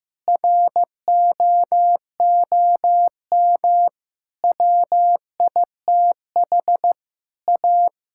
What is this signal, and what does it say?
Morse code, heard as a keyed tone, about 15 wpm, 700 Hz. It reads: ROOM WITH A
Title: Sentences from Top 1600 Words 15wpm